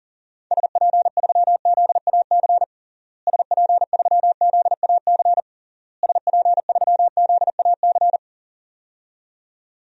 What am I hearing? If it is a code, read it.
Morse code, heard as a keyed tone, about 40 wpm, 700 Hz. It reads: SP3ZAC SP3ZAC SP3ZAC
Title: DXCC-100 Call Signs - 6 Characters - Repeated 3x - 40wpm